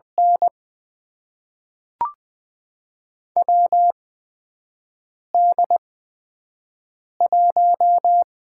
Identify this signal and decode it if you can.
Morse code, heard as a keyed tone, about 20 wpm, spaced as for 5 wpm, 700 Hz. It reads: N WD1
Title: USA Call Signs - 4 Characters - 8x Character Spacing - 20wpm